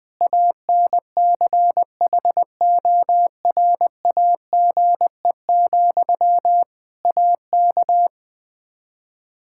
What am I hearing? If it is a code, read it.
Morse code, heard as a keyed tone, about 21 wpm, 700 Hz. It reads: ANCHORAGE, AK
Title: K1USN SST QSO Practice 20wpm